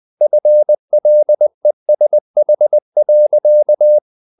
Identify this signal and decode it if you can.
Morse code, heard as a keyed tone, about 20 wpm, 600 Hz. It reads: FLESH.
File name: the old man and the sea / the old man and the sea part 1 20wpm